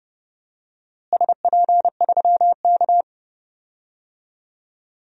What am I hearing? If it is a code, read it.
Morse code, heard as a keyed tone, about 29 wpm, 700 Hz. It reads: SP3K
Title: DXCC-100 Call Signs - 4 Characters - Repeated 3x - 30wpm